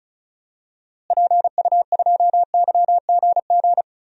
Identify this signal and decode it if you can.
Morse code, heard as a keyed tone, about 36 wpm, 700 Hz. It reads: PU2YGG
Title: DXCC-100 Call Signs - 6 Characters - Repeated 3x - 35wpm